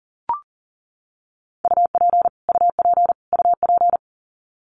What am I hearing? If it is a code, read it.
Morse code, heard as a keyed tone, about 40 wpm, 700 Hz. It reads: UP UP UP
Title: Top 500 Words Repeated 40wpm